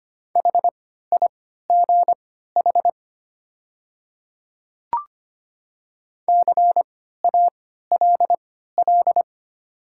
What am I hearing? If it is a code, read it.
Morse code, heard as a keyed tone, about 25 wpm, spaced as for 14 wpm, 700 Hz. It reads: HIGH CALL
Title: 3x Character Spacing - Top 200 Words - 25wpm